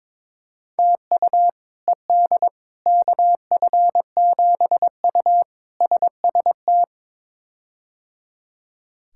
Morse code, heard as a keyed tone, about 22 wpm, 700 Hz. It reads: TU ED KF7U SST
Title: K1USN SST QSO Practice - 22wpm